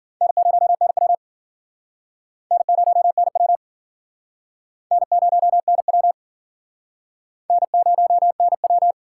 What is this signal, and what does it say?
Morse code, heard as a keyed tone, about 50 wpm, 700 Hz. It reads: N0NW N0NW N0NW N0NW